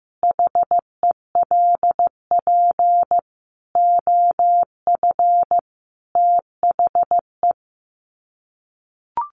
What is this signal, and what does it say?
Morse code, heard as a keyed tone, about 15 wpm, 700 Hz. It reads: HELP OF THE
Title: Sets of 5 Words Encore 15wpm